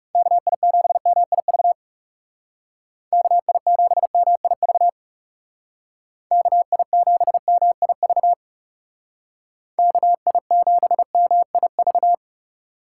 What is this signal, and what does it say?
Morse code, heard as a keyed tone, about 39 wpm, 700 Hz. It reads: KI7MIV KI7MIV KI7MIV KI7MIV